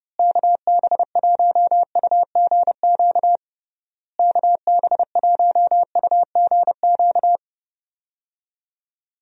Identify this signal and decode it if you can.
Morse code, heard as a keyed tone, about 30 wpm, 700 Hz. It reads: KB1UGQ KB1UGQ
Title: USA Call Signs - 6 Characters - Repeated 3x - 30wpm